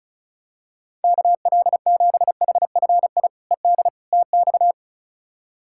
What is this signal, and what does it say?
Morse code, heard as a keyed tone, about 36 wpm, 700 Hz. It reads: KL7HFI ED TX
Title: K1USN SST QSO Practice - 35wpm